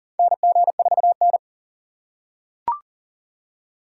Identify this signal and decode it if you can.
Morse code, heard as a keyed tone, about 40 wpm, 700 Hz. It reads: NG4N
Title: USA Call Signs - 4 Characters - 6x Character Spacing - 40wpm